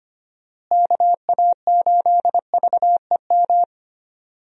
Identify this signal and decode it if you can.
Morse code, heard as a keyed tone, about 25 wpm, 700 Hz. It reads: KA8VEM
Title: USA Call Signs - 6 Characters - Repeated 2x - 25wpm